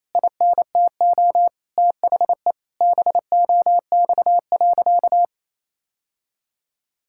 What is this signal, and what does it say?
Morse code, heard as a keyed tone, about 27 wpm, 700 Hz. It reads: INTO THE BOX.
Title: Sentences from Top 1000 Words - Review Part 6 - 28wpm